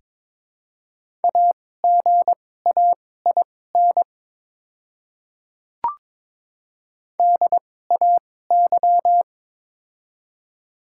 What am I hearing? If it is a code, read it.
Morse code, heard as a keyed tone, about 22 wpm, spaced as for 16 wpm, 700 Hz. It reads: AGAIN DAY
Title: 2x Character Spacing - Top 200 Words - 22wpm